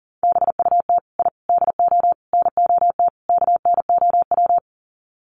Morse code, heard as a keyed tone, about 40 wpm, 700 Hz. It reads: BUT I DO NOT KNOW
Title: Sets of 5 Words 40wpm